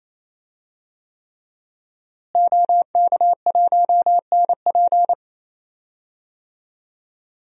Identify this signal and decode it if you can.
Morse code, heard as a keyed tone, about 28 wpm, 700 Hz. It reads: OK1NP
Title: DXCC-100 Call Signs - 5 Characters - Repeated 3x - 28wpm